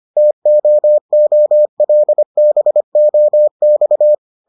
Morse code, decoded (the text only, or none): TOOLBOX